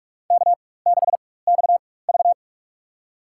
Morse code, heard as a keyed tone, about 46 wpm, spaced as for 21 wpm, 700 Hz. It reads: K6XV
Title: USA Call Signs - 4 Characters - 4x Character Spacing - 45wpm